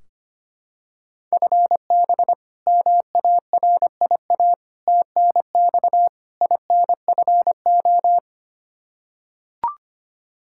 Morse code, heard as a keyed tone, about 25 wpm, 700 Hz. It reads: FB MARIA TNX INFO
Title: Practice QSO - Exchange 3 - More Info - 25wpm